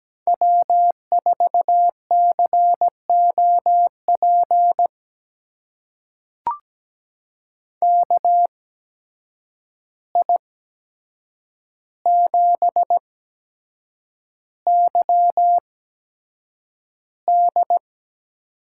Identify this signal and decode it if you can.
Morse code, heard as a keyed tone, about 17 wpm, spaced as for 7 wpm, 700 Hz. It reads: W4COP KI7YD
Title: USA Call Signs - 8x Character Spacing - 17wpm